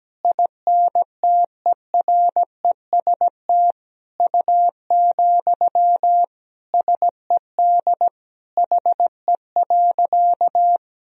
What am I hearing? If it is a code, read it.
Morse code, heard as a keyed tone, about 17 wpm, 700 Hz. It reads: INTEREST U, SED HE.
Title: Sentences from Top 400 Words and Abbreviations 17wpm